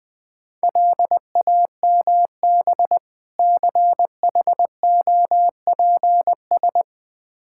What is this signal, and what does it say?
Morse code, heard as a keyed tone, about 21 wpm, 700 Hz. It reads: LAMB CHOPS
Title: Difficult Words with Hints - 20wpm